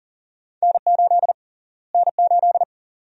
Morse code, heard as a keyed tone, about 40 wpm, 700 Hz. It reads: N8 N8
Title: USA Call Sign Prefixes - Repeated 2x - 40wpm